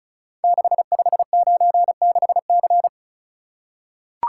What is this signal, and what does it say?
Morse code, heard as a keyed tone, about 36 wpm, 690 Hz. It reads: 6596C
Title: Code Groups - 5 Letter:Numbers - 35wpm